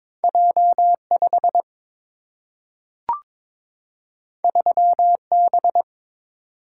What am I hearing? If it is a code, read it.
Morse code, heard as a keyed tone, about 22 wpm, 700 Hz. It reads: J 5 3 B
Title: Single-Letter Single-Number 22wpm